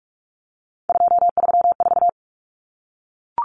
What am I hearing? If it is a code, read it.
Morse code, heard as a keyed tone, about 44 wpm, 700 Hz. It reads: 234
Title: Numbers - 3-Digit 45wpm